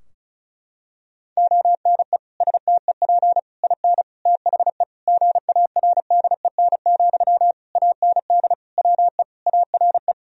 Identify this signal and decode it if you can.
Morse code, heard as a keyed tone, about 34 wpm, 700 Hz. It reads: ONE STEP IN THE GARDEN, AND WE ARE
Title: Sentences from Top 1000 Words - Review Part 4 - 35wpm